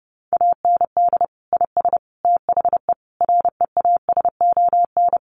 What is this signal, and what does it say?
Morse code, heard as a keyed tone, about 30 wpm, 700 Hz. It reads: AND IS THE REASON
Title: Sentences from Top 700 Words 30wpm